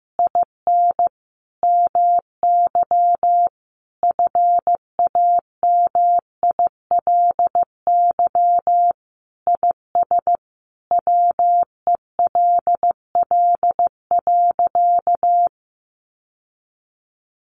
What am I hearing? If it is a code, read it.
Morse code, heard as a keyed tone, about 14 wpm, 700 Hz. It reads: IN MY FAMILY IS WELL.
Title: Sentences from Top 800 Words 15wpm